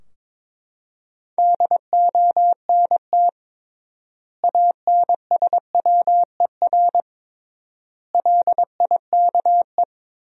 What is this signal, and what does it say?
Morse code, heard as a keyed tone, about 21 wpm, 700 Hz. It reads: DONT ANSWER LIKE
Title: 3x Word-Spacing - Sentences from Top 400 Words - 22wpm